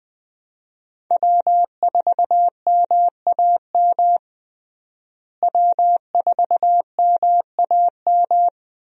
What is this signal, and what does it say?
Morse code, heard as a keyed tone, about 20 wpm, 700 Hz. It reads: W4MAM W4MAM
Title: USA Call Signs - 5 Characters - Repeated 2x - 20wpm